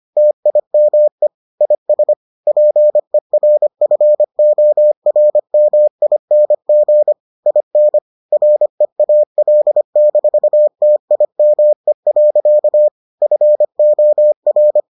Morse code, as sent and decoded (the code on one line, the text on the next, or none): - .. -- . / .. ... / .--. .-. ..-. --- .-. -- .. -. --. / .. -. / .-. .- .-.. -....- - .. -- . .-.-.- / ..-. --- .-.
TIME IS PERFORMING IN REAL-TIME. FOR